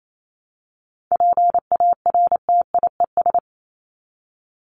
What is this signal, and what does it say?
Morse code, heard as a keyed tone, about 28 wpm, 700 Hz. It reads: PARTIES
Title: ES:S Suffix - Top 100 Words with Base Words 28wpm